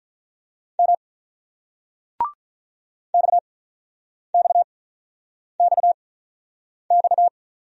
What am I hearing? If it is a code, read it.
Morse code, heard as a keyed tone, about 45 wpm, spaced as for 9 wpm, 700 Hz. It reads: M XXXX